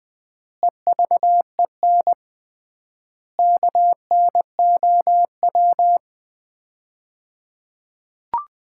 Morse code, heard as a keyed tone, about 20 wpm, 700 Hz. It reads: EVEN KNOW
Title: 3x Word-Spacing - Sets of 4 Words - 20wpm